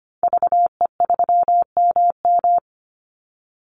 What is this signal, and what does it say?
Morse code, heard as a keyed tone, about 25 wpm, 700 Hz. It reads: VE3MM
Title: Call Signs - 25wpm